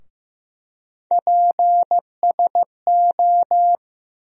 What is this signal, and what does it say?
Morse code, heard as a keyed tone, about 15 wpm, 700 Hz. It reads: PSO
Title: USA Call Sign Postfixes - 3 Letters 15wpm